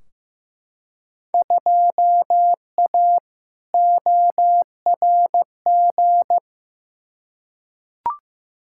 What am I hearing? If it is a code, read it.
Morse code, heard as a keyed tone, about 15 wpm, 700 Hz. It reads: 2A ORG
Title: Field Day Exchanges Encore - 15wpm